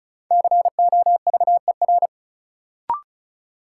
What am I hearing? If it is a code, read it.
Morse code, heard as a keyed tone, about 36 wpm, 700 Hz. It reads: COVER
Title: Sentences from Top 500 Words - New Word Drill - 35wpm